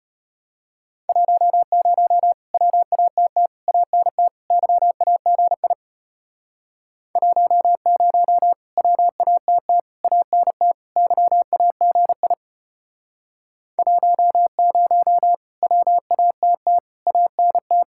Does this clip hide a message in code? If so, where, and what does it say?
Yes: Morse code from 1.1 to 17.8 s, heard as a keyed tone, about 32 wpm, 700 Hz. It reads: 10 WATT ANT YAGI 10 WATT ANT YAGI 10 WATT ANT